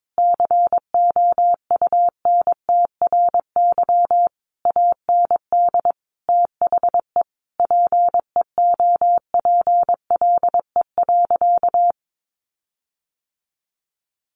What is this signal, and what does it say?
Morse code, heard as a keyed tone, about 22 wpm, 700 Hz. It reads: COUNTRY AND THE PEOPLE.
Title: Sentences from Top 1500 Words 22wpm